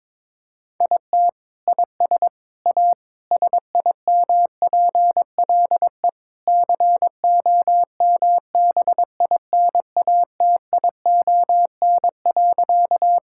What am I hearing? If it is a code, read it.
Morse code, heard as a keyed tone, about 21 wpm, 700 Hz. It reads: IT IS A SIMPLE COMBINATION.
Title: Sentences from Top 1000 Words - Review Part 6 - 22wpm